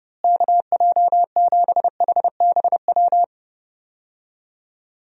Morse code, heard as a keyed tone, about 30 wpm, 700 Hz. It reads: KJ7HBW
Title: USA Call Signs - 6 Characters - Repeated 3x - 30wpm